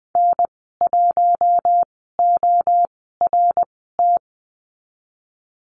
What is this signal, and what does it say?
Morse code, heard as a keyed tone, about 20 wpm, spaced as for 15 wpm, 700 Hz. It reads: N1ORT
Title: 128 Lesson 21g - Number 6 - Call Signs 20wpm